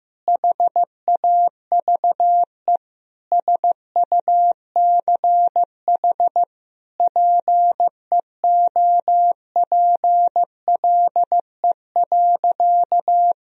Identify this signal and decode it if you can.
Morse code, heard as a keyed tone, about 15 wpm, 700 Hz. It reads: HAVE SUCH PEOPLE.
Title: Sentences from Top 1000 Words - Review Part 3 - 15wpm